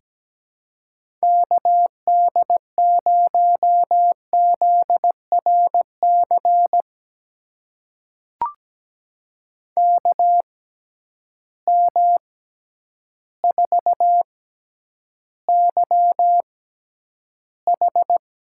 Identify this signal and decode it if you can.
Morse code, heard as a keyed tone, about 17 wpm, spaced as for 9 wpm, 700 Hz. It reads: KD0ZRC KM4YH
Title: USA Call Signs - 6 Characters - 6x Character Spacing - 17wpm